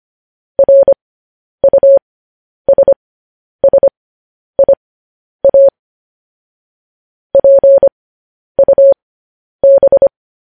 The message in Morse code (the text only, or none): RUSSIA PUB